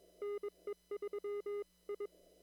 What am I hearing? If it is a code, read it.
Morse code, heard as a keyed tone, about 21 wpm, 410 Hz. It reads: NE3I
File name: CW / kt5j wpx cw-op-k5pi